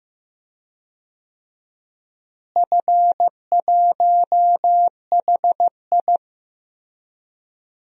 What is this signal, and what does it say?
Morse code, heard as a keyed tone, about 15 wpm, 700 Hz. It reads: F1HI
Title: DXCC-100 Call Signs - 4 Characters - Repeated 3x - 15wpm